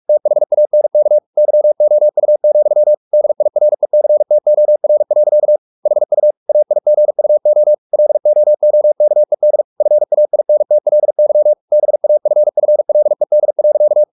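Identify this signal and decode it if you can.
Morse code, heard as a keyed tone, about 44 wpm, 600 Hz. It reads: THANK YOU, DIRECTOR. HU AIGUO LOOKED FAINTLY BAFFLED.
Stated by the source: Morse practice at 45 wpm, the novel Never by Ken Follett in English, from CW med Gnister.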